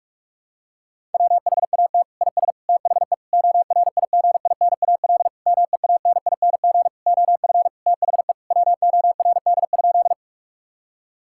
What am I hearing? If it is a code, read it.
Morse code, heard as a keyed tone, about 46 wpm, 700 Hz. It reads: WHAT IS THE ORIGINAL MEANING OF THE WORD?
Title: Sentences from Top 1000 Words - Review Part 6 - 45wpm